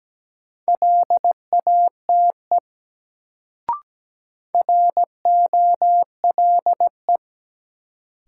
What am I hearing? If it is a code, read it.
Morse code, heard as a keyed tone, about 17 wpm, 700 Hz. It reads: LATE ROLE
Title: Sets of 3 Words - New Word Drill - 17wpm